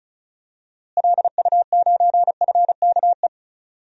Morse code, heard as a keyed tone, about 36 wpm, 700 Hz. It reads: LU9FKE
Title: DXCC-100 Call Signs - 6 Characters - Repeated 3x - 35wpm